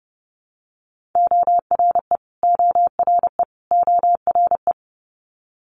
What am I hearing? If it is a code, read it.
Morse code, heard as a keyed tone, about 30 wpm, 700 Hz. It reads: ORE ORE ORE
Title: Multi-letter Phonemes Repeated 3x 30wpm